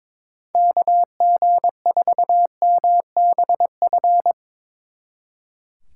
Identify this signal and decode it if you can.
Morse code, heard as a keyed tone, about 24 wpm, 700 Hz. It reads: KG4MBF
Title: Speed Racing - USA 2x3 Call Signs 22wpm